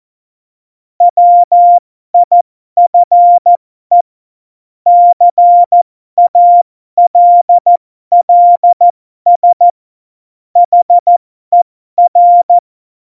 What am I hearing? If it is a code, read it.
Morse code, heard as a keyed tone, about 14 wpm, spaced as for 12 wpm, 700 Hz. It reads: WIFE CALLS HER
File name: Morsekurs / Witze